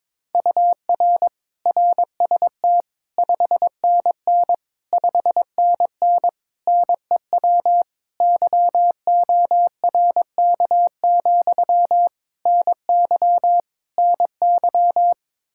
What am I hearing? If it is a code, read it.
Morse code, heard as a keyed tone, about 22 wpm, 700 Hz. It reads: UR RST 5NN 5NN NEW YORK, NY NY
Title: Practice QSO - Exchange 1 - Beginning - 22wpm